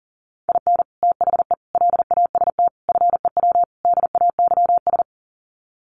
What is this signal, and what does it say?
Morse code, heard as a keyed tone, about 40 wpm, 700 Hz. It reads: IN THE LAST FEW DAYS
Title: Sets of 5 Words Encore2 40wpm